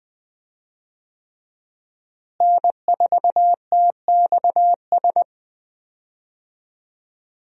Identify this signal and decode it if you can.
Morse code, heard as a keyed tone, about 20 wpm, 700 Hz. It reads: N4TXS
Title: USA Call Signs - 5 Characters - Repeated 3x - 20wpm